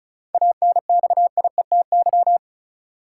Morse code, heard as a keyed tone, about 34 wpm, 700 Hz. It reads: ANXIETY
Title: Sentences from Top 1600 Words - New Word Drill - 35wpm